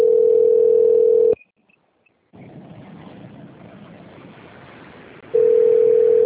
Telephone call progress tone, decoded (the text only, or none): ringback tone